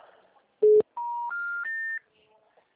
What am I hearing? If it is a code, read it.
Special information tone.